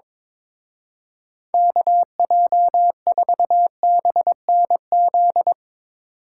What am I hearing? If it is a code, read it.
Morse code, heard as a keyed tone, about 22 wpm, 700 Hz. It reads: KJ4BNZ
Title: Speed Racing - USA 2x3 Call Signs 22wpm